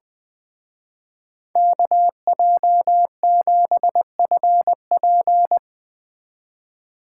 Morse code, heard as a keyed tone, about 19 wpm, 700 Hz. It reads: KJ7FP
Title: Speed Racing - USA 2x2 Call Signs 17wpm